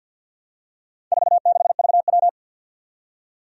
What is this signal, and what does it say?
Morse code, heard as a keyed tone, about 51 wpm, 700 Hz. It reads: 46VW
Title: Four Challenging Letter-Number Combinations 50wpm